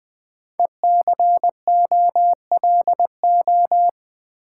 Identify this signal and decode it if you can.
Morse code, heard as a keyed tone, about 20 wpm, 700 Hz. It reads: ECOLO
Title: Build A Word - 5 Letters At A Time - 10-Letter Words - 20wpm